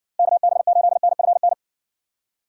Morse code, heard as a keyed tone, about 55 wpm, 700 Hz. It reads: KB8NFN